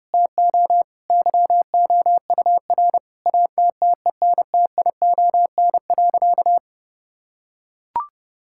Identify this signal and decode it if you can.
Morse code, heard as a keyed tone, about 32 wpm, 700 Hz. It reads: TO YOUR ATTENTION.
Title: Sentences from Top 1000 Words - Review Part 6 - 30wpm